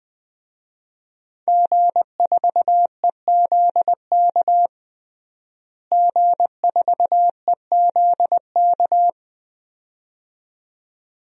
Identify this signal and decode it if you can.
Morse code, heard as a keyed tone, about 20 wpm, 700 Hz. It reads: G4EZK G4EZK
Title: DXCC-100 Call Signs - 5 Characters - Repeated 2x - 20wpm